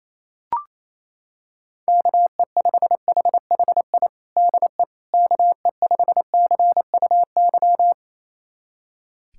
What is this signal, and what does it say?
Morse code, heard as a keyed tone, about 28 wpm, 700 Hz. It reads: KE5HHI DE KE5CUY
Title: USA Callsign DE Callsign - 6 Characters - 28wpm